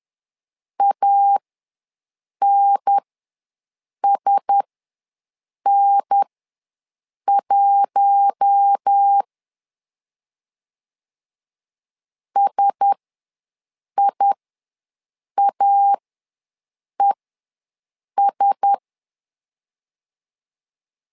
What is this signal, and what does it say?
Morse code, heard as a keyed tone, about 10 wpm, spaced as for 6 wpm, 800 Hz. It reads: ANSN1 SIAES
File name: LEZIONI / 4 Q1S ws 10wpm 2208